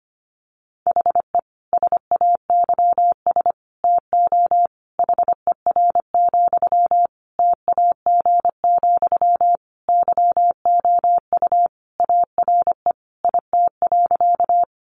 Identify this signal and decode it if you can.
Morse code, heard as a keyed tone, about 25 wpm, 700 Hz. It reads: HE SAYS TO HER, TAG, YOU ARE IT.
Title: Additional Easy Phrases 25wpm